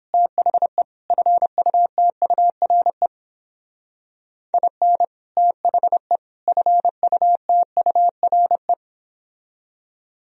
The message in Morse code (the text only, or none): THE FUTURE IN THE FUTURE